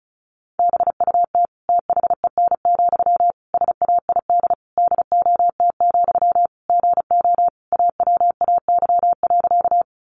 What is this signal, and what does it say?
Morse code, heard as a keyed tone, about 34 wpm, 700 Hz. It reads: BUT THEN, SAID DOT, GO AWAY.
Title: Sentences from Top 1500 Words 35wpm